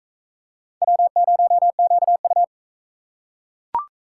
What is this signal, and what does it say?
Morse code, heard as a keyed tone, about 40 wpm, 700 Hz. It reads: W0QU